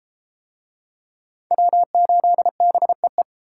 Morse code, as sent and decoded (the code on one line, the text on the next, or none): .-- ---.. -... .
W8BEE